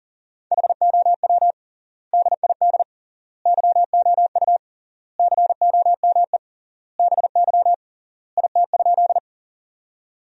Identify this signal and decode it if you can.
Morse code, heard as a keyed tone, about 40 wpm, 700 Hz. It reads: HOW DID YOU COME BY IT?
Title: 3x Word-Spacing - Sentences from Top 100 Words - 40wpm